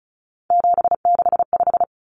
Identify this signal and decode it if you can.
Morse code, heard as a keyed tone, about 34 wpm, 700 Hz. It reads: 765
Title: Numbers - 3-Digit 35wpm